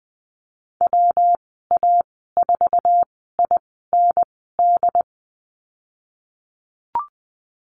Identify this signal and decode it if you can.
Morse code, heard as a keyed tone, about 20 wpm, spaced as for 15 wpm, 700 Hz. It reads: WA4IND
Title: 128 Lesson 21g - Number 6 - Call Signs 20wpm